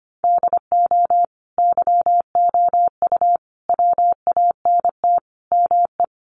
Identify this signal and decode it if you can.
Morse code, heard as a keyed tone, about 25 wpm, 700 Hz. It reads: DO YOU WANT ME
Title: Sentences from Top 600 Words 25wpm